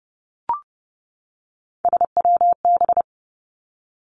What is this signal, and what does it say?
Morse code, heard as a keyed tone, about 30 wpm, 700 Hz. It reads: SWB
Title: Three Challenging Letter-Number Combinations 30wpm